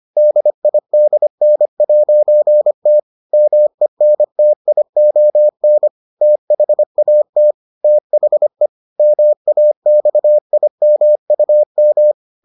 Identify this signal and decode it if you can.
Morse code, heard as a keyed tone, about 25 wpm, 600 Hz. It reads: DIDN'T MENTION THAT THE MAXIMUM